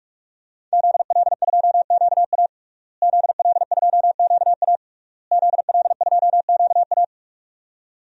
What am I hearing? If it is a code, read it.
Morse code, heard as a keyed tone, about 46 wpm, 700 Hz. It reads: ZL2QA ZL2QA ZL2QA
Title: DXCC-100 Call Signs - 5 Characters - Repeated 3x - 45wpm